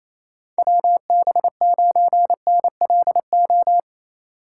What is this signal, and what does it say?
Morse code, heard as a keyed tone, about 28 wpm, 700 Hz. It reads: WB9NLO
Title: USA Call Signs - 8x Character Spacing - 28wpm